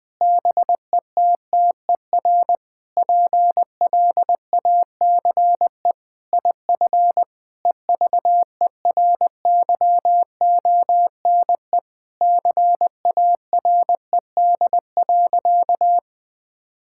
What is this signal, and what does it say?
Morse code, heard as a keyed tone, about 20 wpm, 700 Hz. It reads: BETTER PLACE IF EVERYONE CARED.
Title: Sentences with 8-Word Phrases - 20wpm